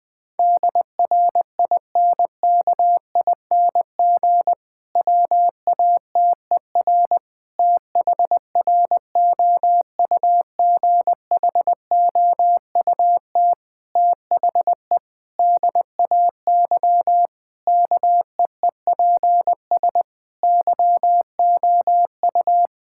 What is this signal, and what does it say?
Morse code, heard as a keyed tone, about 20 wpm, 700 Hz. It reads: DRINKING WATER THROUGHOUT THE DAY KEEPS YOU
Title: Difficult Sentences with Hints - 20wpm